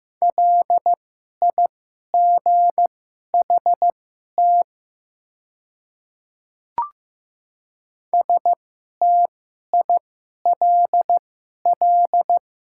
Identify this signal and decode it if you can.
Morse code, heard as a keyed tone, about 15 wpm, spaced as for 11 wpm, 700 Hz. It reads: LIGHT STILL
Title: 2x Character Spacing - Top 300 Words - 15wpm